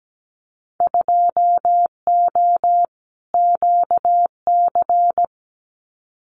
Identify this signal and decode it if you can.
Morse code, heard as a keyed tone, about 17 wpm, 700 Hz. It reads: 2O QC
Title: Winter Field Day Exchanges 17wpm